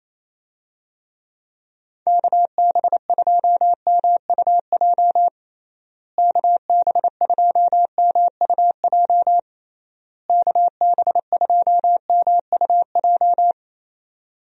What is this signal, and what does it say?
Morse code, heard as a keyed tone, about 28 wpm, 700 Hz. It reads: KB2MUJ KB2MUJ KB2MUJ
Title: USA Call Signs - 6 Characters - Repeated 3x - 28wpm